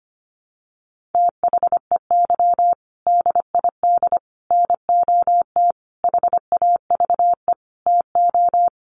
Morse code, heard as a keyed tone, about 25 wpm, 700 Hz. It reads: THEY DID NOT HAVE TO
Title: Sets of 5 Words Encore2 25wpm